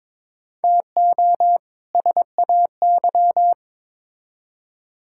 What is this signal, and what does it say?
Morse code, heard as a keyed tone, about 21 wpm, 700 Hz. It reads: TO SAY
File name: Build A Phrase - 2-Word Phrases - 22wpm